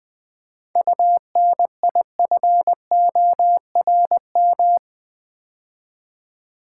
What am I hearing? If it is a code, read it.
Morse code, heard as a keyed tone, about 20 wpm, 700 Hz. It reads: UNIFORM
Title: 7 Letter Words - First 3 Letters Given - 20wpm